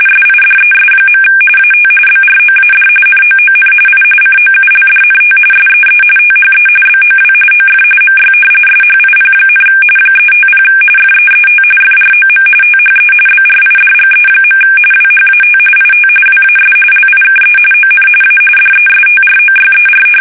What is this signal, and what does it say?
RTTY, about 45.45 baud. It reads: SIDES. AT BROAD NOON I COULDNT SEE THE BOW OF MY SHIP, 350 FEET FROM THE BRIDGE. WHAT IT WAS LIKE ON A DESTROYER
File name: fsk850 / battleshipnj